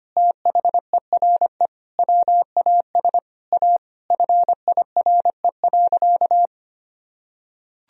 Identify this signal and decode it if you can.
Morse code, heard as a keyed tone, about 25 wpm, 700 Hz. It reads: THERE WAS A FIRE.
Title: Sentences from Top 1000 Words - Review Part 4 - 25wpm